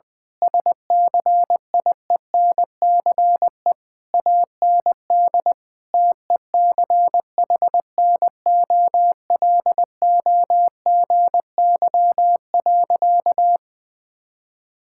Morse code, heard as a keyed tone, about 20 wpm, 700 Hz. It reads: SCIENCE AND TECHNOLOGY.
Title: Sentences from Top 1000 Words - Review Part 6 - 20wpm